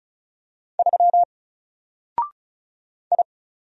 Morse code, heard as a keyed tone, about 34 wpm, 700 Hz. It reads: 3 I
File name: Single-Letter-Number-35wpm